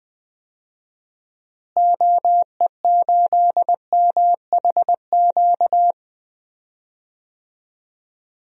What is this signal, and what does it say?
Morse code, heard as a keyed tone, about 20 wpm, 700 Hz. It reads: OE8MHQ
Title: DXCC-100 Call Signs - 6 Characters - Repeated 2x - 20wpm